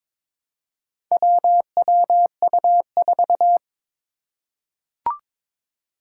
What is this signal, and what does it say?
Morse code, heard as a keyed tone, about 22 wpm, 700 Hz. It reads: WWU4
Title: Four Challenging Letter-Number Combinations 22wpm